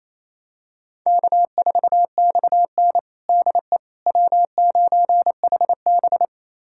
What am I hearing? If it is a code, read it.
Morse code, heard as a keyed tone, about 28 wpm, 700 Hz. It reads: K4XN DE W9HB
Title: USA Callsign DE Callsign - 4 Characters - 28wpm